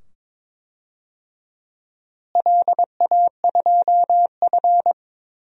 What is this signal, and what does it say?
Morse code, heard as a keyed tone, about 22 wpm, 700 Hz. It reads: LA2F
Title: DXCC-100 Call Signs - 4 Characters - Repeated 2x - 22wpm